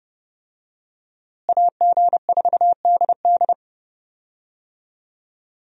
Morse code, heard as a keyed tone, about 30 wpm, 700 Hz. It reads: AG4DD
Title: USA Call Signs - 5 Characters - Repeated 3x - 30wpm